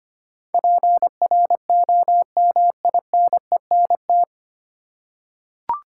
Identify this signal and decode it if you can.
Morse code, heard as a keyed tone, about 25 wpm, 700 Hz. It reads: PROMINENT
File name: Nine Letter Words - 25wpm